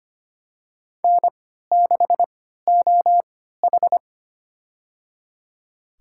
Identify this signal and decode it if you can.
Morse code, heard as a keyed tone, about 25 wpm, spaced as for 14 wpm, 700 Hz. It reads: N6OH
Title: USA Call Signs - 4 Characters - 3x Character Spacing - 25wpm